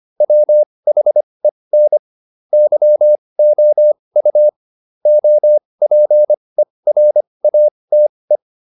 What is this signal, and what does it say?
Morse code, heard as a keyed tone, about 25 wpm, spaced as for 20 wpm, 600 Hz. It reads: WHEN YOU OPERATE